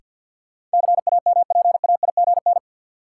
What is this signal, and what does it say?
Morse code, heard as a keyed tone, about 51 wpm, 700 Hz. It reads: CAMPAIGN
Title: Sentences from Top 1200 Words - New Word Drill - 50wpm